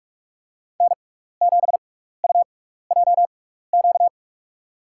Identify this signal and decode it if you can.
Morse code, heard as a keyed tone, about 43 wpm, spaced as for 15 wpm, 700 Hz. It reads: N7UJQ
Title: USA Call Signs - 5 Characters - 6x Character Spacing - 45wpm